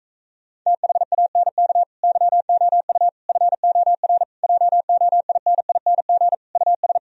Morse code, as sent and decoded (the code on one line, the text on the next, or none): - .... .- -. -.- / -.-- --- ..- / ..-. --- .-. / .--- --- .. -. .. -. --. / ..- ...
THANK YOU FOR JOINING US